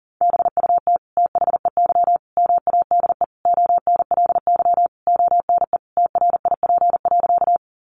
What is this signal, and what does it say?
Morse code, heard as a keyed tone, about 37 wpm, 700 Hz. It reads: BUT THEY MADE ONLY ONE TRIP.
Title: Sentences from Top 1300 Words 40wpm